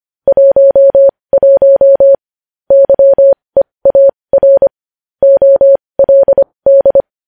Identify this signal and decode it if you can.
Morse code, heard as a keyed tone, about 25 wpm, spaced as for 20 wpm, 550 Hz. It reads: 11 YEAR OLD
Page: news headlines in morse code at 20 wpm